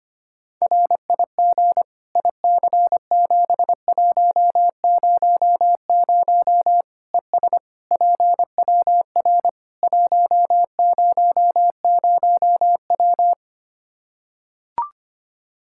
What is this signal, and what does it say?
Morse code, heard as a keyed tone, about 25 wpm, 700 Hz. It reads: RIG IC7100 ES PWR 100W
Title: K1USN SST QSO Practice 25wpm